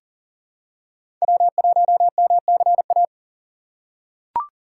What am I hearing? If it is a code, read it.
Morse code, heard as a keyed tone, about 40 wpm, 700 Hz. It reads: W1MCA
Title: USA Call Signs - 5 Characters - 2x Character Spacing - 40wpm